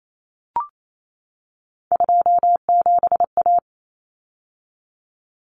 Morse code, heard as a keyed tone, about 28 wpm, 700 Hz. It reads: 27A
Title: Common Cut Numbers - 3-Digit 28wpm